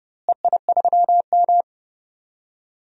Morse code, heard as a keyed tone, about 30 wpm, 700 Hz. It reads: EI3M
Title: DXCC-100 Call Signs - 4 Characters - Repeated 3x - 30wpm